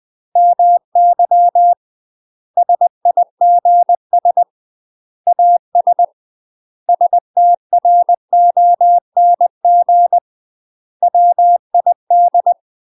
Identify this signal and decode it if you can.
Morse code, heard as a keyed tone, about 20 wpm, 700 Hz. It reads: MY SIGS AS STRONG WID
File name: UK MASTER QSOs TRAINING / 20WPM / UK MASTER QSOs 0030